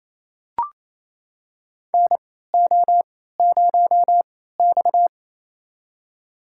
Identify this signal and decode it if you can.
Morse code, heard as a keyed tone, about 27 wpm, spaced as for 16 wpm, 700 Hz. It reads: NO0X
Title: USA Call Signs - 4 Characters - 3x Character Spacing - 28wpm